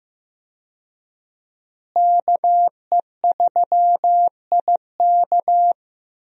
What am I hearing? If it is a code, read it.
Morse code, heard as a keyed tone, about 15 wpm, 700 Hz. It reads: KE3IK
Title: USA Call Signs - 5 Characters - Repeated 3x - 15wpm